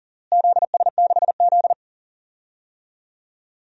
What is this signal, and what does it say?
Morse code, heard as a keyed tone, about 40 wpm, 700 Hz. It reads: ZS6Z
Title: DXCC-100 Call Signs - 4 Characters - Repeated 2x - 40wpm